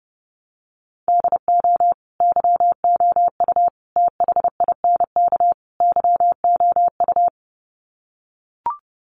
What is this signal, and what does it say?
Morse code, heard as a keyed tone, about 30 wpm, 700 Hz. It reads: DO YOU THINK YOU
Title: Sets of 4 Words 30wpm